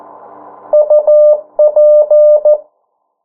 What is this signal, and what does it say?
Morse code, heard as a keyed tone, about 14 wpm, 600 Hz. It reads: UP
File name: morse mp3 / 09 urok 8